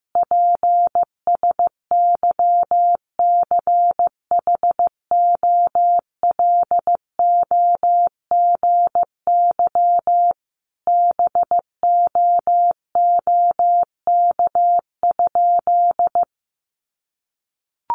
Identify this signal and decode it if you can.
Morse code, heard as a keyed tone, about 15 wpm, 700 Hz. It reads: PSYCHOLOGY BOOK?
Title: Sentences from Top 1400 Words 15wpm